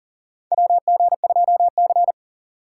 Morse code, heard as a keyed tone, about 40 wpm, 700 Hz. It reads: WG2C
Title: USA Call Signs - 4 Characters - 3x Character Spacing - 40wpm